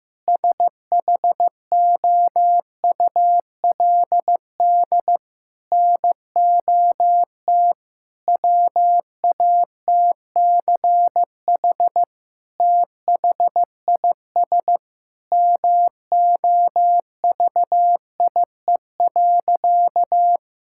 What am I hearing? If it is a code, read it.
Morse code, heard as a keyed tone, about 15 wpm, 700 Hz. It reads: SHOULD NOT WATCH THIS MOVIE.
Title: Sentences from Top 1000 Words - Review Part 5 - 15wpm